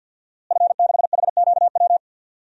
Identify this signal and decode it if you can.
Morse code, heard as a keyed tone, about 52 wpm, 700 Hz. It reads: F6HQW
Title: DXCC-100 Call Signs - 5 Characters - Repeated 2x - 50wpm